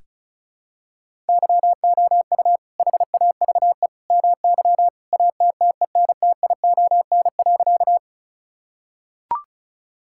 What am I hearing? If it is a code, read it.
Morse code, heard as a keyed tone, about 34 wpm, 700 Hz. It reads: YOU HAVE MY ATTENTION.
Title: Sentences from Top 1000 Words - Review Part 6 - 35wpm